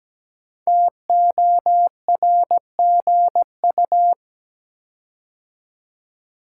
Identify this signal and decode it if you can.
Morse code, heard as a keyed tone, about 17 wpm, 700 Hz. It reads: TORGU
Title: Code Groups - 5 Letters - 17wpm